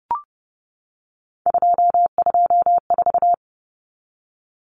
Morse code, heard as a keyed tone, about 30 wpm, 700 Hz. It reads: 224
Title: Common Cut Numbers - 3-Digit 30wpm